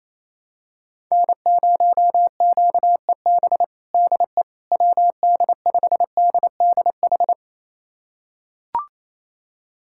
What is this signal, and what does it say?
Morse code, heard as a keyed tone, about 28 wpm, 700 Hz. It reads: N0QEB DE WD5DDH
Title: USA Callsign DE Callsign 28wpm